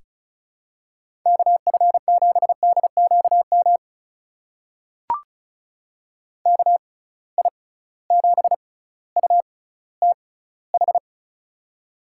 Morse code, heard as a keyed tone, about 35 wpm, spaced as for 18 wpm, 700 Hz. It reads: KF7DQM KI7UTH
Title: USA Call Signs - 6 Characters - 6x Character Spacing - 35wpm